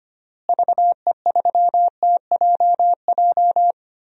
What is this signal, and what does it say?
Morse code, heard as a keyed tone, about 25 wpm, 700 Hz. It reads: VE3TJJ
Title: DXCC-100 Call Signs - 6 Characters - Repeated 3x - 25wpm